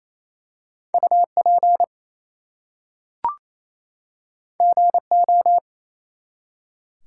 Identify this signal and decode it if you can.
Morse code, heard as a keyed tone, about 27 wpm, 700 Hz. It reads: UP GO
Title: Two Letter Words - 28wpm